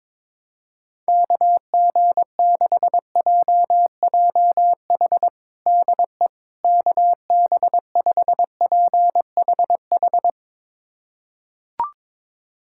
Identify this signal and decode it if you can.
Morse code, heard as a keyed tone, about 22 wpm, 700 Hz. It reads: KG6JJH DE KB5PHH
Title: USA Callsign DE Callsign 22wpm